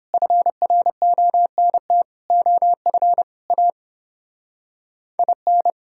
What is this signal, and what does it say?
Morse code, heard as a keyed tone, about 29 wpm, 700 Hz. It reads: FRONT OF A IN